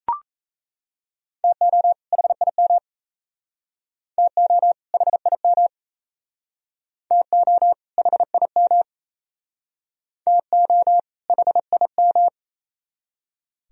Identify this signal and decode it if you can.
Morse code, heard as a keyed tone, about 37 wpm, 700 Hz. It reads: TO HIM TO HIM TO HIM TO HIM